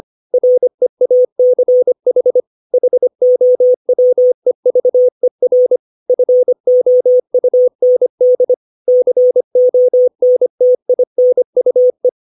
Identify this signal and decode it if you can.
Morse code, heard as a keyed tone, about 25 wpm, 500 Hz. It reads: REACH HOWEVER FOUND CONTINUE
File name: 632 words / 632 Words 25WPM